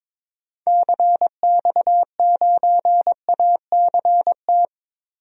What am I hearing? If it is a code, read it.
Morse code, heard as a keyed tone, about 22 wpm, 700 Hz. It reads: CX9ACT
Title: DXCC-100 Call Signs - 6 Characters - Repeated 2x - 22wpm